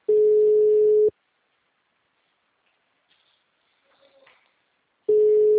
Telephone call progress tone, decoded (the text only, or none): ringback tone